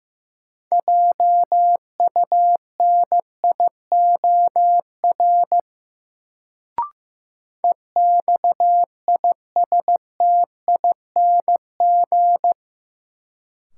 Morse code, heard as a keyed tone, about 15 wpm, 700 Hz. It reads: JUNIOR EXISTING
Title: Sentences from Top 1500 Words - New Word Drill Alt - 15wpm